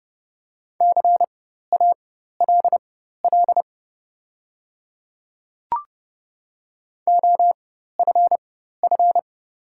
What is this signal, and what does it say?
Morse code, heard as a keyed tone, about 30 wpm, spaced as for 14 wpm, 700 Hz. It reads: CALL OFF
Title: 4x Character Spacing - Top 200 Words - 30wpm